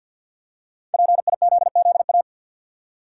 Morse code, heard as a keyed tone, about 52 wpm, 700 Hz. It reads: PIZZA
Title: Five Letter Words - 50wpm